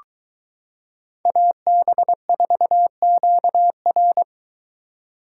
Morse code, heard as a keyed tone, about 23 wpm, 700 Hz. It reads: AB4QR